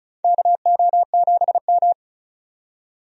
Morse code, heard as a keyed tone, about 36 wpm, 700 Hz. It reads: KO7M
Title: USA Call Signs - 4 Characters - 35wpm